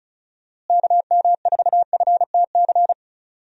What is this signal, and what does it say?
Morse code, heard as a keyed tone, about 34 wpm, 700 Hz. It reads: KM4FTC
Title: USA Call Signs - 6 Characters - Repeated 3x - 35wpm